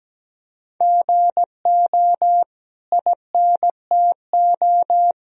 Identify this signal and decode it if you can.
Morse code, heard as a keyed tone, about 17 wpm, 700 Hz. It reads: GO INTO
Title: Sentences from Top 1000 Words - Review Part 4 - 17wpm